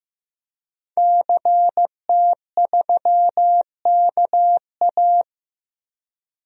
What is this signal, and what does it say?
Morse code, heard as a keyed tone, about 15 wpm, 700 Hz. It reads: CT3KA
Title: DXCC-100 Call Signs - 5 Characters - Repeated 3x - 15wpm